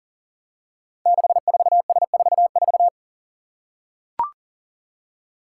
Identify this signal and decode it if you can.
Morse code, heard as a keyed tone, about 38 wpm, 690 Hz. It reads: 64S44
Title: Five Challenging Letter-Number Combinations 40wpm